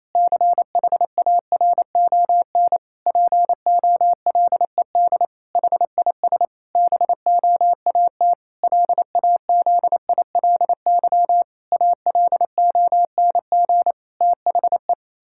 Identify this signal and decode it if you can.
Morse code, heard as a keyed tone, about 28 wpm, 700 Hz. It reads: CHARON POLED HIS BOAT LAZILY ALONG THE
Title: Pangram Challenge v2 - 28wpm